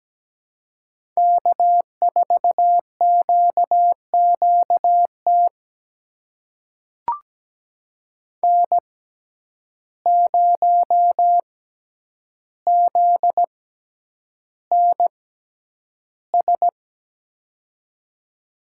Morse code, heard as a keyed tone, about 17 wpm, spaced as for 9 wpm, 700 Hz. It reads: K4QQT N0ZNS
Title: USA Call Signs - 5 Characters - 6x Character Spacing - 17wpm